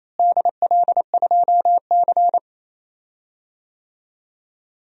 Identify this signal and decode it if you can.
Morse code, heard as a keyed tone, about 27 wpm, 700 Hz. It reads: DL2C
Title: DXCC-100 Call Signs - 4 Characters - Repeated 2x - 28wpm